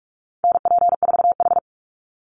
Morse code, heard as a keyed tone, about 44 wpm, 700 Hz. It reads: NP4H
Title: Call Signs Repeated 3x - 45wpm